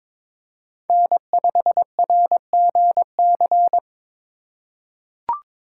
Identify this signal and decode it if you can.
Morse code, heard as a keyed tone, about 22 wpm, 700 Hz. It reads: N5RGC
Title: USA Call Signs - 5 Characters - 2x Character Spacing - 22wpm